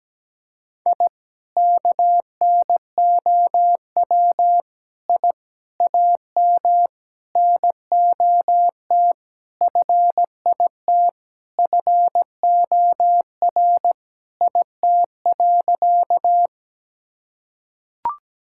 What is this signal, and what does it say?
Morse code, heard as a keyed tone, about 17 wpm, 700 Hz. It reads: I KNOW I AM NOT FIT FOR IT.
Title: Sentences from Top 1000 Words - Review Part 5 - 17wpm